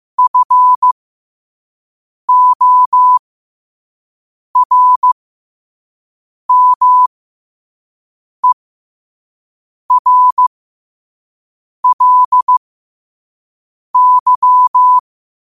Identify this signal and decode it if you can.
Morse code, heard as a keyed tone, about 15 wpm, spaced as for 5 wpm, 1000 Hz. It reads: FORMERLY
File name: qotdfiles / 5wpm